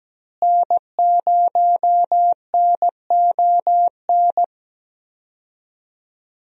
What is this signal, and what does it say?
Morse code, heard as a keyed tone, about 17 wpm, 700 Hz. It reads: N0NON
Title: USA Call Signs - 5 Characters - Repeated 3x - 17wpm